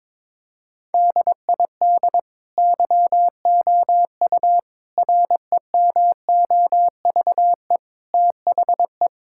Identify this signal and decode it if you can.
Morse code, heard as a keyed tone, about 22 wpm, 700 Hz. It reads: DID YOU REMOVE THE
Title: Sentences from Top 1000 Words - Review Part 5 - 22wpm